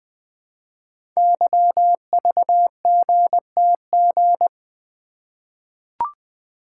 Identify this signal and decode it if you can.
Morse code, heard as a keyed tone, about 20 wpm, 700 Hz. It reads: YVGTG
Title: Code Groups - 5 Letters - 20wpm